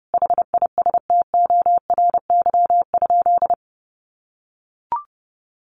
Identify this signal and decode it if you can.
Morse code, heard as a keyed tone, about 28 wpm, 700 Hz. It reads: HISTORY?
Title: Sentences from Top 600 Words 30wpm